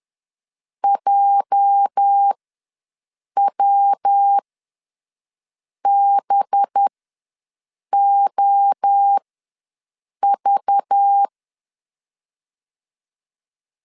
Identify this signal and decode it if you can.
Morse code, heard as a keyed tone, about 10 wpm, spaced as for 6 wpm, 800 Hz. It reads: JWBOV
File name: LEZIONI / 8 YJ45 / 8 YJ45 ws 10wpm 3843